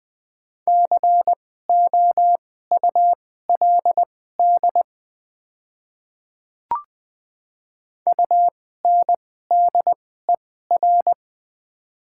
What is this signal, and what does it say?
Morse code, heard as a keyed tone, about 21 wpm, spaced as for 15 wpm, 700 Hz. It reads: COULD UNDER
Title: 2x Character Spacing - Top 500 Words - 20wpm